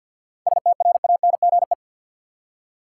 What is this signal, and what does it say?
Morse code, heard as a keyed tone, about 46 wpm, 700 Hz. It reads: STRANGE
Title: Seven Letter Words - 50wpm